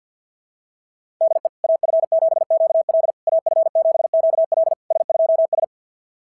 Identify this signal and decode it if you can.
Morse code, heard as a keyed tone, about 46 wpm, 660 Hz. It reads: DE AF7QL AF7QL S2S
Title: SOTA QSO Practice 50wpm